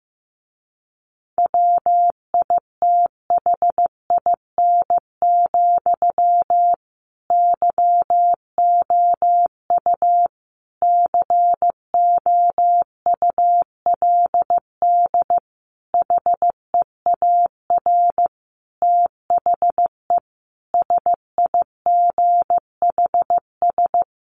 Morse code, heard as a keyed tone, about 15 wpm, 700 Hz. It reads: WITHIN, YOU COULD HEAR THE SIGHS
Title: A Sound Of Thunder 15wpm